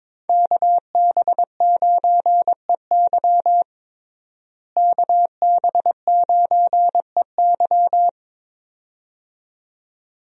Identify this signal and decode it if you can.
Morse code, heard as a keyed tone, about 22 wpm, 700 Hz. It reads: KB9EY KB9EY
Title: USA Call Signs - 5 Characters - Repeated 2x - 22wpm